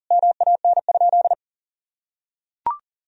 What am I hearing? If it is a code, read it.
Morse code, heard as a keyed tone, about 40 wpm, 700 Hz. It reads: MAN?
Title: Sentences from Top 1000 Words - Review Part 3 - 40wpm